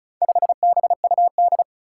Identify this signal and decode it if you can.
Morse code, heard as a keyed tone, about 36 wpm, 700 Hz. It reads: 5BUD
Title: Four Challenging Letter-Number Combinations 35wpm